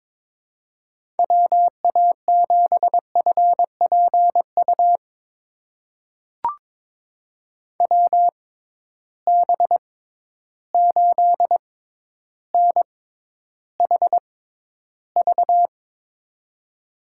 Morse code, heard as a keyed tone, about 22 wpm, spaced as for 11 wpm, 700 Hz. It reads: WA7FPU WB8NHV
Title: USA Call Signs - 6 Characters - 6x Character Spacing - 22wpm